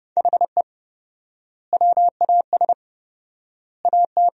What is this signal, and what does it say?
Morse code, heard as a keyed tone, about 31 wpm, 700 Hz. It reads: HE WAS AT
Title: 4x Word-Spacing - Sentences from Top 200 Words - 30wpm